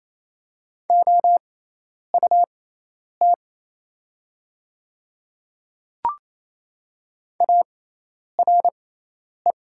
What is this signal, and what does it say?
Morse code, heard as a keyed tone, about 27 wpm, spaced as for 10 wpm, 700 Hz. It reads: OUT ARE
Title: 6x Character Spacing - Top 100 Words - 28wpm